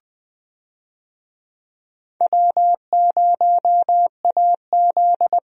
Morse code, heard as a keyed tone, about 20 wpm, 700 Hz. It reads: W0AZ
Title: USA Call Signs - 4 Characters - Repeated 3x - 20wpm